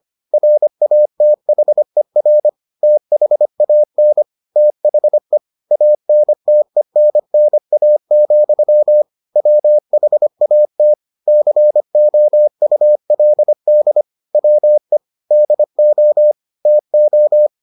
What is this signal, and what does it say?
Morse code, heard as a keyed tone, about 25 wpm, 600 Hz. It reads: RATHER THAN THE ANTENNA, WHAT COULD WE DO TO